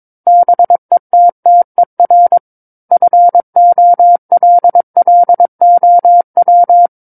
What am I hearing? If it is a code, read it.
Morse code, heard as a keyed tone, about 22 wpm, 700 Hz. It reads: BETTER FOLLOW
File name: Book One / 13-Chapter 13 - 22 WPM